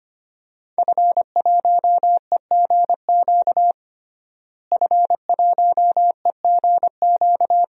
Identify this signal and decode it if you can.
Morse code, heard as a keyed tone, about 25 wpm, 700 Hz. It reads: F1EGQ F1EGQ
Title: DXCC-100 Call Signs - 5 Characters - Repeated 2x - 25wpm